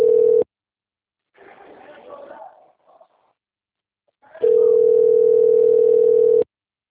Ringback tone (North America), 440 + 480 Hz.
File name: out- - - - - .11882